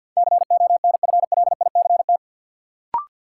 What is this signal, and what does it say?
Morse code, heard as a keyed tone, about 52 wpm, 700 Hz. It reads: CONFLICT
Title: Sentences from Top 1300 Words - New Word Drill - 50wpm